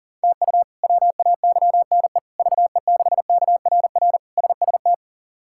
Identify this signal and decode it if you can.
Morse code, heard as a keyed tone, about 40 wpm, 700 Hz. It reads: TU WAYNE VE6KRR SST
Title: K1USN SST QSO Practice - 40wpm